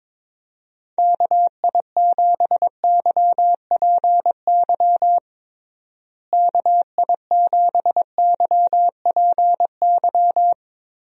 Morse code, heard as a keyed tone, about 22 wpm, 700 Hz. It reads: KI7YPY KI7YPY
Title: USA Call Signs - 6 Characters - Repeated 2x - 22wpm